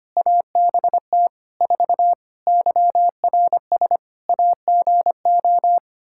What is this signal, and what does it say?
Morse code, heard as a keyed tone, about 25 wpm, 700 Hz. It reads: ABT 4 YRS AGO
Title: Practice QSO - Exchange 3 - More Info - 25wpm